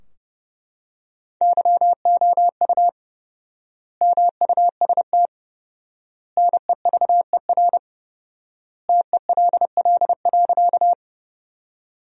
Morse code, heard as a keyed tone, about 32 wpm, 700 Hz. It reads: YOU MUST NEVER TELL.
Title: 4x Word-Spacing - Sentences from Top 300 Words - 30wpm